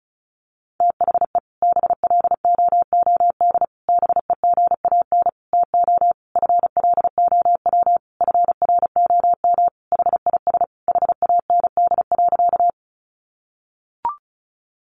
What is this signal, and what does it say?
Morse code, heard as a keyed tone, about 34 wpm, 700 Hz. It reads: THE BLOOD BEGAN TO FLOW FROM HIS HAND.
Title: Sentences from Top 1200 Words 35wpm